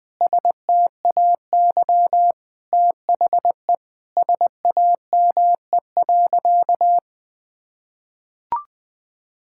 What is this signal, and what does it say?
Morse code, heard as a keyed tone, about 20 wpm, 700 Hz. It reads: STAY THE SAME.
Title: Sentences with 8-Word Phrases - 20wpm